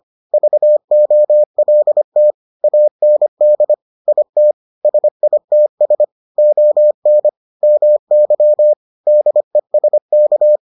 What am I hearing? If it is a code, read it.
Morse code, heard as a keyed tone, about 25 wpm, 600 Hz. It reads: VOLT AND IT SITS ON MY DESK